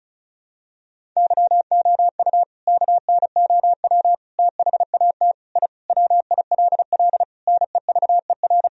Morse code, heard as a keyed tone, about 36 wpm, 700 Hz. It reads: YOU KNOW THAT I WILL NEVER
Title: Sentences from Top 1000 Words - Review Part 1 - 35wpm